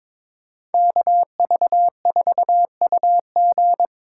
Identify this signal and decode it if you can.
Morse code, heard as a keyed tone, about 26 wpm, 700 Hz. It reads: KV4UG
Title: Speed Racing - USA 2x2 Call Signs 22wpm